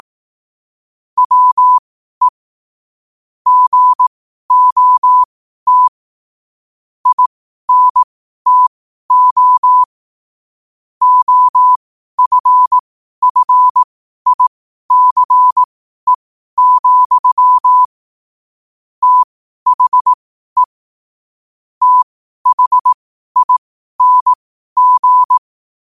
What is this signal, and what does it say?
Morse code, heard as a keyed tone, about 18 wpm, spaced as for 12 wpm, 1000 Hz. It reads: WE GOT INTO OFFICE, THE THING
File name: qotdfiles / 12wpm